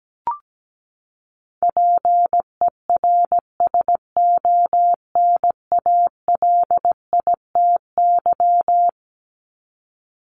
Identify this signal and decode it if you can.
Morse code, heard as a keyed tone, about 17 wpm, 700 Hz. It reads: PERSONALITY
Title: Top 1251-1500 Words 17wpm